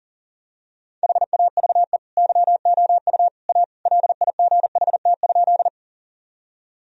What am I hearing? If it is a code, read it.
Morse code, heard as a keyed tone, about 40 wpm, 700 Hz. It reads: HAVE YOU A LIGHT?
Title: Sentences from Top 1000 Words - Review Part 1 - 40wpm